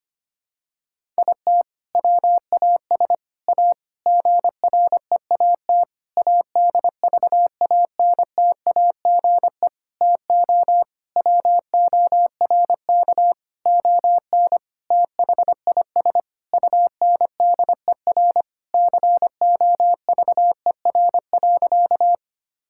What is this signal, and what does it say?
Morse code, heard as a keyed tone, about 25 wpm, 700 Hz. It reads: IT WAS A GREAT ADVANTAGE TO WORK ON THIS UNDER COVER.
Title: Sentences from Top 1000 Words - Review Part 6 - 25wpm